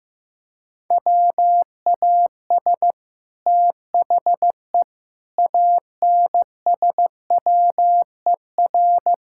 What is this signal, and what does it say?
Morse code, heard as a keyed tone, about 15 wpm, 700 Hz. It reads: WAS THE ANSWER
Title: Sentences from Top 1000 Words - Review Part 6 - 15wpm